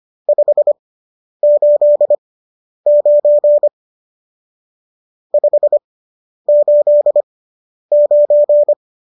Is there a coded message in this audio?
Yes: Morse code from 0.3 to 8.7 s, heard as a keyed tone, about 25 wpm, spaced as for 10 wpm, 590 Hz. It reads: 589 589